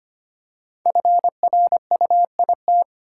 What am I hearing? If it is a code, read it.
Morse code, heard as a keyed tone, about 25 wpm, 700 Hz. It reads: FRUIT
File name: 5 Letter Words - First 3 Letters Given - 25wpm